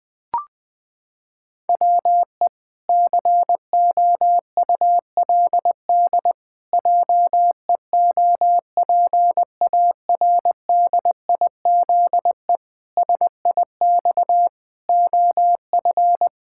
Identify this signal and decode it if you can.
Morse code, heard as a keyed tone, about 20 wpm, 700 Hz. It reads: WE COULD JEOPARDIZE SIX OF
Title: Pangram Challenge v2 - 20wpm